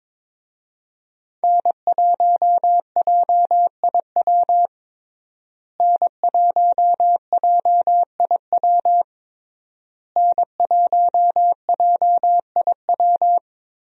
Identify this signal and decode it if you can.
Morse code, heard as a keyed tone, about 21 wpm, 700 Hz. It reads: N1JIW N1JIW N1JIW
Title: USA Call Signs - 5 Characters - Repeated 3x - 22wpm